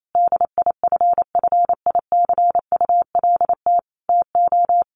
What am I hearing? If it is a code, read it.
Morse code, heard as a keyed tone, about 28 wpm, 700 Hz. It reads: DIFFICULT TO
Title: Sentences from Top 1100 Words 28wpm